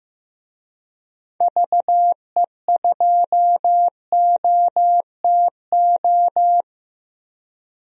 Morse code, heard as a keyed tone, about 16 wpm, 700 Hz. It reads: VE2OTO
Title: DXCC-100 Call Signs - 6 Characters - 15wpm